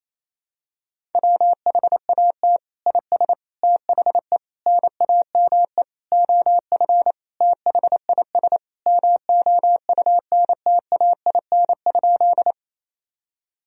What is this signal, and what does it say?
Morse code, heard as a keyed tone, about 28 wpm, 700 Hz. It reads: WHAT IS THE NAME OF THIS MOUNTAIN?
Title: Sentences from Top 1000 Words - Review Part 1 - 28wpm